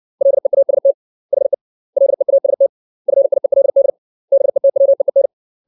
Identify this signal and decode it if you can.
Morse code, heard as a keyed tone, about 60 wpm, 550 Hz. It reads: LEAST HE LEAST FIELD BETWEEN
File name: 632 words / 632 Words 60WPM